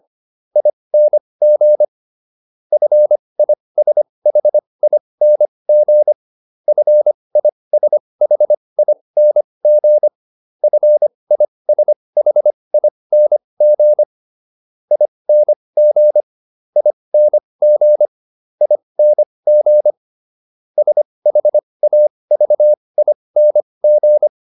Morse code, heard as a keyed tone, about 25 wpm, spaced as for 20 wpm, 600 Hz. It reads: ING FISHING FISHING FISHING ING ING ING SHAVING